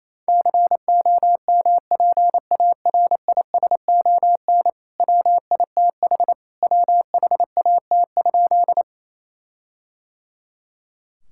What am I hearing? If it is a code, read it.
Morse code, heard as a keyed tone, about 28 wpm, 700 Hz. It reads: COMPARISON WITH WHAT?
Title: Sentences from Top 1600 Words 28wpm